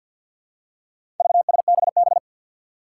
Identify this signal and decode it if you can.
Morse code, heard as a keyed tone, about 48 wpm, 700 Hz. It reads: VSBB
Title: Four Challenging Letter-Number Combinations 50wpm